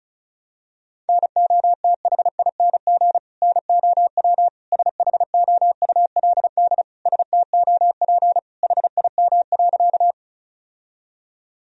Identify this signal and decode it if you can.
Morse code, heard as a keyed tone, about 36 wpm, 700 Hz. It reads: NOTHING NOW SHOULD STOP HIM.
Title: Sentences from Top 1000 Words - Review Part 1 - 35wpm